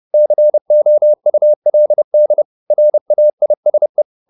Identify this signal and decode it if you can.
Morse code, heard as a keyed tone, about 30 wpm, 600 Hz. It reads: COULD RAISE